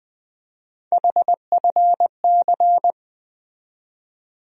Morse code, heard as a keyed tone, about 20 wpm, 700 Hz. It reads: HFC
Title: USA Call Sign Postfixes - 3 Letters 20wpm